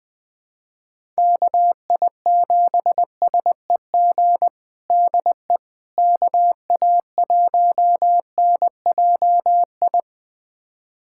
Morse code, heard as a keyed tone, about 20 wpm, 700 Hz. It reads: KI7SEG DE KA1NJI
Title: USA Callsign DE Callsign - 6 Characters - 20wpm